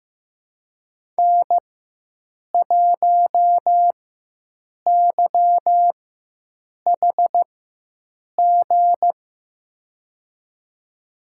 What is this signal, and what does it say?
Morse code, heard as a keyed tone, about 15 wpm, spaced as for 7 wpm, 700 Hz. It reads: N1YHG
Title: USA Call Signs - 5 Characters - 4x Character Spacing - 15wpm